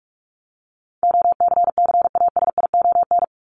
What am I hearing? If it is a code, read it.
Morse code, heard as a keyed tone, about 44 wpm, 700 Hz. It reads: OCCASION
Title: Top 1501-1750 Words 45wpm